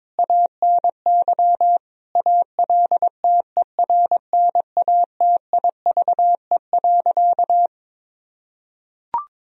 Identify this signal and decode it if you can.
Morse code, heard as a keyed tone, about 22 wpm, 700 Hz. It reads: ANY ALTERNATIVE.
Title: Sentences from Top 1000 Words - Review Part 4 - 22wpm